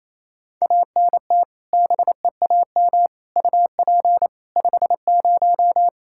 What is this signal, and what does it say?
Morse code, heard as a keyed tone, about 28 wpm, 700 Hz. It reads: ANT BEAM UP 50
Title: Practice QSO - Exchange 2 - Info - 28wpm